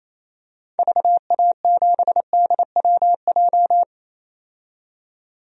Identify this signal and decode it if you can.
Morse code, heard as a keyed tone, about 28 wpm, 700 Hz. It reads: VA7DWJ
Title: DXCC-100 Call Signs - 6 Characters - 28wpm